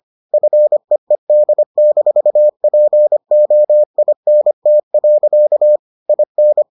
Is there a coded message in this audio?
Yes: Morse code from 0.3 to 6.6 s, heard as a keyed tone, about 25 wpm, 600 Hz. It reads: FEED-POINT. IN